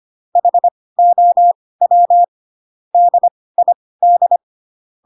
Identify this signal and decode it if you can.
Morse code, heard as a keyed tone, about 25 wpm, spaced as for 18 wpm, 700 Hz. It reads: HOW DID
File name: Pinocchio / Pinocchio ch7 12 25 ik1hge